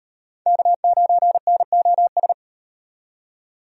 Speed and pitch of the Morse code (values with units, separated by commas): 35 wpm, 700 Hz